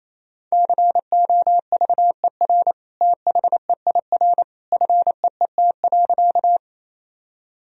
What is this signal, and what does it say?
Morse code, heard as a keyed tone, about 28 wpm, 700 Hz. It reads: COVER THEIR FEET.
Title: Sentences from Top 1000 Words - Review Part 1 - 28wpm